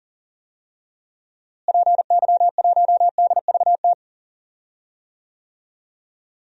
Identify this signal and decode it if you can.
Morse code, heard as a keyed tone, about 37 wpm, 700 Hz. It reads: PY1DVT
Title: DXCC-100 Call Signs - 6 Characters - Repeated 3x - 40wpm